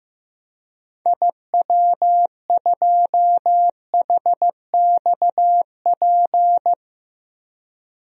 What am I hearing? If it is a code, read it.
Morse code, heard as a keyed tone, about 15 wpm, 700 Hz. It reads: IW2HXP
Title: DXCC-100 Call Signs - 6 Characters - 15wpm